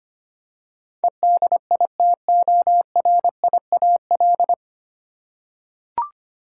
Morse code, heard as a keyed tone, about 25 wpm, 700 Hz. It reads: EDITORIAL
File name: Nine Letter Words - 25wpm